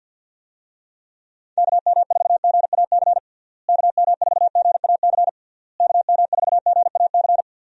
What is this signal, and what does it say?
Morse code, heard as a keyed tone, about 50 wpm, 700 Hz. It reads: KM4GAC KM4GAC KM4GAC
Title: USA Call Signs - 6 Characters - Repeated 3x - 50wpm